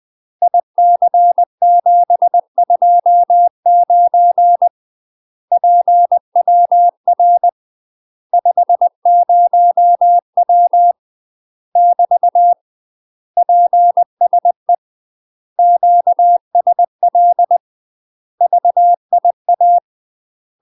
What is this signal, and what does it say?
Morse code, heard as a keyed tone, about 20 wpm, 700 Hz. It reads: IC729 PWR 50W <BT> PSE QSL VIA
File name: UK MASTER QSOs TRAINING / 20WPM / UK MASTER QSOs 0068